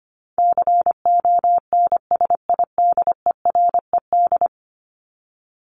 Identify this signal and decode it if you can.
Morse code, heard as a keyed tone, about 25 wpm, 700 Hz. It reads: CONSIDERED
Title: ED Suffix - Top 100 Words with Base Words 25wpm